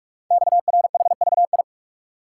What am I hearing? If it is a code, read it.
Morse code, heard as a keyed tone, about 44 wpm, 700 Hz. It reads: XRHVI
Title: Code Groups - 5 Letters - 45wpm